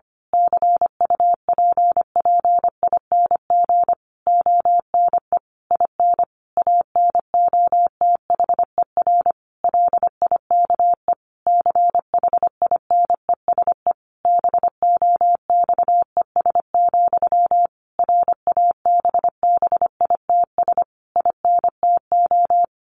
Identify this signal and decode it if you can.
Morse code, heard as a keyed tone, about 25 wpm, 700 Hz. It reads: CUPPING ONE IN ANOTHER LIKE CHINESE BOXES, RABBITS INTO
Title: A Sound Of Thunder 25wpm